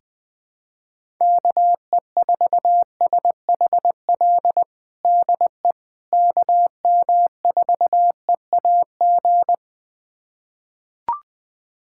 Morse code, heard as a keyed tone, about 21 wpm, 700 Hz. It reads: KE4SHL DE KM4EAG
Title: USA Callsign DE Callsign - 6 Characters - 20wpm